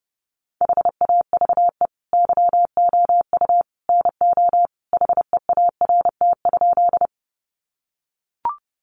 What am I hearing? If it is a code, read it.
Morse code, heard as a keyed tone, about 30 wpm, 700 Hz. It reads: HAVE YOU NO HEART?
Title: Sentences from Top 1200 Words 30wpm